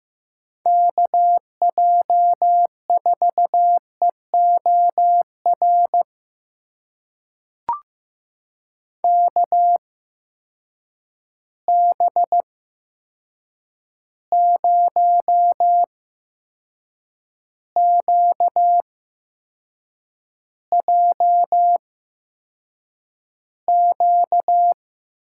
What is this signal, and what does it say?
Morse code, heard as a keyed tone, about 15 wpm, spaced as for 6 wpm, 700 Hz. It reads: KJ4EOR KB0QJQ
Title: USA Call Signs - 6 Characters - 8x Character Spacing - 15wpm